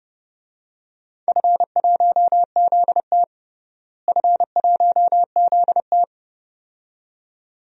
Morse code, heard as a keyed tone, about 30 wpm, 700 Hz. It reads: F1ZT F1ZT
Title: DXCC-100 Call Signs - 4 Characters - Repeated 2x - 30wpm